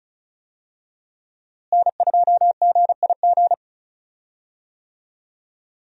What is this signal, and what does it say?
Morse code, heard as a keyed tone, about 36 wpm, 700 Hz. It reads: N2GIG
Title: USA Call Signs - 5 Characters - Repeated 3x - 35wpm